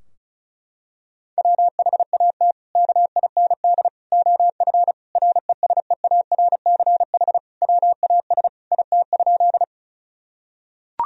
Morse code, heard as a keyed tone, about 34 wpm, 700 Hz. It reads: WHAT KIND OF RESEARCH WAS IT?
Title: Sentences from Top 1000 Words - Review Part 4 - 35wpm